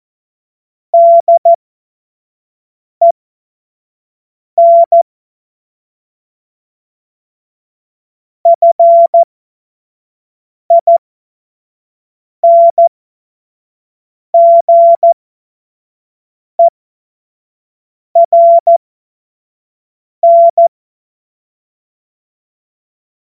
Morse code, heard as a keyed tone, about 14 wpm, spaced as for 5 wpm, 680 Hz. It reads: DEN FINGERN
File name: Morsekurs / Witze / Witz 14. Dezember 2021, 25 F70